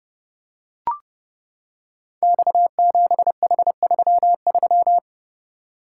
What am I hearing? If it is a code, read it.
Morse code, heard as a keyed tone, about 30 wpm, 700 Hz. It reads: X7H33
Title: Code Groups - 5 Letter:Numbers - 30wpm